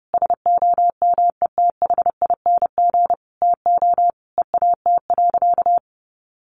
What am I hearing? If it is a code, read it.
Morse code, heard as a keyed tone, about 30 wpm, 700 Hz. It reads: SOMETHING TO EAT.
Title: Sentences from Top 600 Words 30wpm